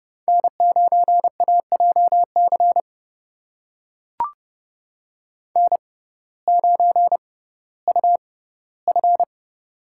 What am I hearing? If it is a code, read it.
Morse code, heard as a keyed tone, about 28 wpm, spaced as for 16 wpm, 700 Hz. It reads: N9AJC N9UF
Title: USA Call Signs - 6x Character Spacing - 30wpm